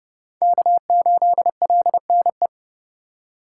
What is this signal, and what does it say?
Morse code, heard as a keyed tone, about 30 wpm, 700 Hz. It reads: K8LNE
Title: USA Call Signs - 5 Characters - 2x Character Spacing - 30wpm